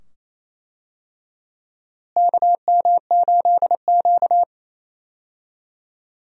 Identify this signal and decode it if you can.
Morse code, heard as a keyed tone, about 28 wpm, 700 Hz. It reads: KM8Q
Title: USA Call Signs - 4 Characters - Repeated 2x - 28wpm